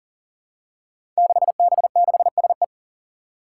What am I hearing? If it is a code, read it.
Morse code, heard as a keyed tone, about 40 wpm, 700 Hz. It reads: 6B6SE
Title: Five Challenging Letter-Number Combinations 40wpm